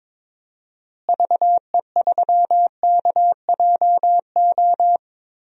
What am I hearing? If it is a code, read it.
Morse code, heard as a keyed tone, about 22 wpm, 700 Hz. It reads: VE3KJO
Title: DXCC-100 Call Signs - 6 Characters - Repeated 2x - 22wpm